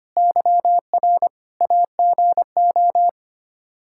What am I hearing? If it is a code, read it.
Morse code, heard as a keyed tone, about 25 wpm, 700 Hz. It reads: YR AGO
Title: Practice QSO - Exchange 3 - More Info - 25wpm